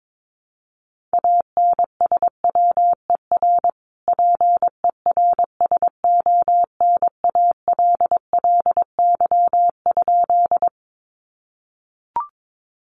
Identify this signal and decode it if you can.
Morse code, heard as a keyed tone, about 22 wpm, 700 Hz. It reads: ANSWER PERSONALLY?
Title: Sentences from Top 1400 Words 22wpm